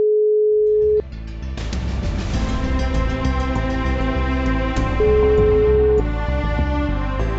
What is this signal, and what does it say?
Ringback tone (Europe, 425 Hz), with noise.